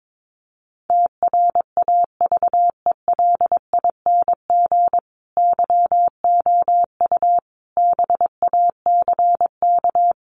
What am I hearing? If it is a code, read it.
Morse code, heard as a keyed tone, about 22 wpm, 700 Hz. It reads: TRAVELING YOU BACK
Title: A Sound Of Thunder 22wpm